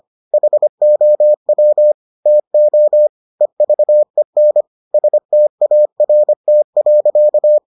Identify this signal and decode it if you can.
Morse code, heard as a keyed tone, about 25 wpm, 600 Hz. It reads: HOW TO EVEN START.